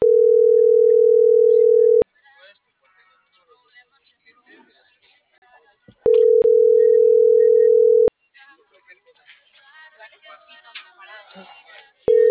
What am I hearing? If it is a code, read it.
Ringback tone (North America), 440 + 480 Hz.